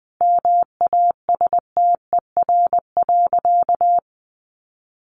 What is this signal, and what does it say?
Morse code, heard as a keyed tone, about 19 wpm, 700 Hz. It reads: MASTER.
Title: Sentences from Top 1300 Words 20wpm